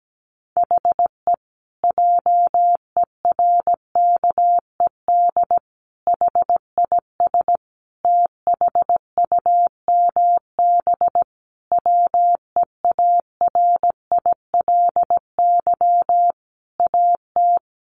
Morse code, heard as a keyed tone, about 17 wpm, 700 Hz. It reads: HE JERKED HIS THUMB WEARILY AT
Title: A Sound Of Thunder 17wpm